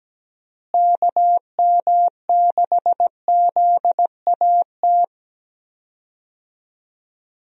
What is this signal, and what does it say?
Morse code, heard as a keyed tone, about 17 wpm, 700 Hz. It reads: KM6ZAT
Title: USA Call Signs - 6 Characters - Repeated 2x - 17wpm